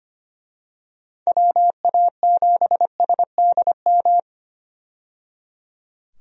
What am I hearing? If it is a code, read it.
Morse code, heard as a keyed tone, about 25 wpm, 700 Hz. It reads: WA7SDM
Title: USA Call Signs - 6 Characters - 25wpm